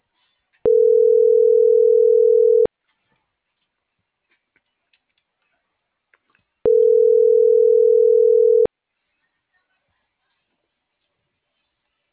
Ringback tone (North America), 440 + 480 Hz.